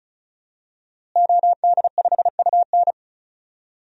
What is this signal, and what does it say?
Morse code, heard as a keyed tone, about 34 wpm, 700 Hz. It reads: OD5UN
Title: DXCC-100 Call Signs - 5 Characters - Repeated 3x - 35wpm